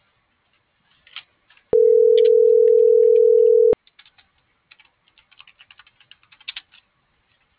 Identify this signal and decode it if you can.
Ringback tone (North America), 440 + 480 Hz.